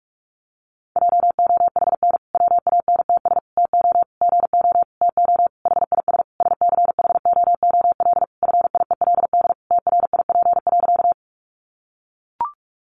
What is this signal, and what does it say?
Morse code, heard as a keyed tone, about 44 wpm, 700 Hz. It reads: JOHN WANTS TO GO TO HIS SCHOOL FIELD TRIP.
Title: Additional Easy Phrases 45wpm